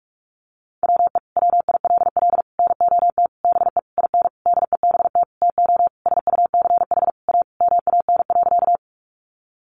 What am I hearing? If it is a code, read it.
Morse code, heard as a keyed tone, about 44 wpm, 700 Hz. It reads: WE WILL NOT BE IN DEBT TO SUCH A MAN.
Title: Sentences from Top 1300 Words 45wpm